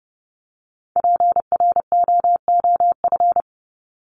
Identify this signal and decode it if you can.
Morse code, heard as a keyed tone, about 30 wpm, 700 Hz. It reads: PROOF
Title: Top 1501-1750 Words 30wpm